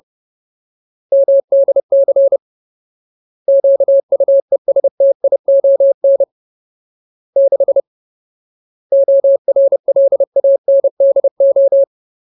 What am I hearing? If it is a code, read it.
Morse code, heard as a keyed tone, about 30 wpm, 550 Hz. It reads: MDC QUESTION 6 ORLANDO
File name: Ultimate Word List / 30WPM Ultimate